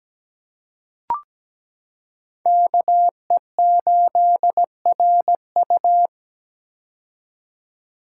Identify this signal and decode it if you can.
Morse code, heard as a keyed tone, about 17 wpm, 700 Hz. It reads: KE8RU
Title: K1USN SST QSO Practice - 17wpm